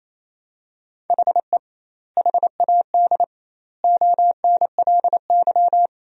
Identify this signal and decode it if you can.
Morse code, heard as a keyed tone, about 28 wpm, 700 Hz. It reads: HE HAD ONLY
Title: 2x Word-Spacing - Sentences from Top 300 Words - 28wpm